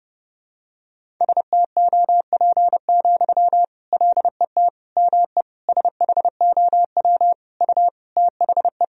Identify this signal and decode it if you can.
Morse code, heard as a keyed tone, about 30 wpm, 700 Hz. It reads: STOP, LET ME SHOW U THE
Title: Sentences from Top 300 Words and Abbreviations 30wpm